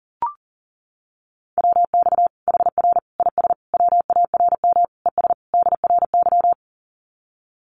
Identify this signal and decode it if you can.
Morse code, heard as a keyed tone, about 40 wpm, 700 Hz. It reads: WX HR IS WARM ES DRY
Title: Common QSO Phrases 40wpm